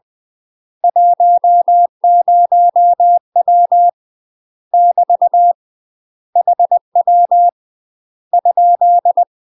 Morse code, heard as a keyed tone, about 20 wpm, 700 Hz. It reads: 10W <BT> HW ?
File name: UK MASTER QSOs TRAINING/20WPM